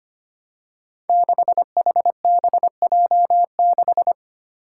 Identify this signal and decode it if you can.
Morse code, heard as a keyed tone, about 25 wpm, 700 Hz. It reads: 6HBJ6
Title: Five Challenging Letter-Number Combinations 25wpm